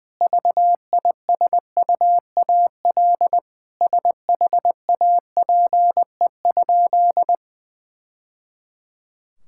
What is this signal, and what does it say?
Morse code, heard as a keyed tone, about 20 wpm, 700 Hz. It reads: VISUAL SHAPE?
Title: Sentences from Top 1000 Words - Review Part 6 - 20wpm